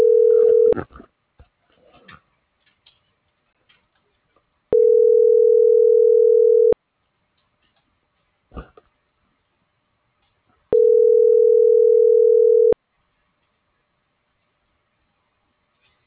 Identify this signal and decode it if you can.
Ringback tone (North America), 440 + 480 Hz.